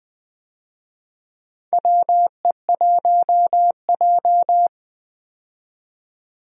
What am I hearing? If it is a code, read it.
Morse code, heard as a keyed tone, about 20 wpm, 700 Hz. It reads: WE1J
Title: USA Call Signs - 4 Characters - 20wpm